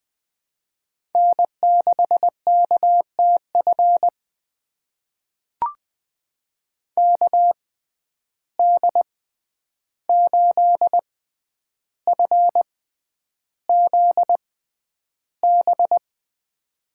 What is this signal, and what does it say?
Morse code, heard as a keyed tone, about 21 wpm, spaced as for 10 wpm, 700 Hz. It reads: N6KTF KD8FZB
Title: USA Call Signs - 6x Character Spacing - 20wpm